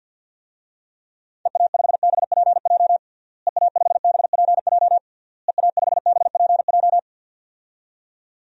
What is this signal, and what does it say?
Morse code, heard as a keyed tone, about 52 wpm, 700 Hz. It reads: EA5BPJ EA5BPJ EA5BPJ
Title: DXCC-100 Call Signs - 6 Characters - Repeated 3x - 50wpm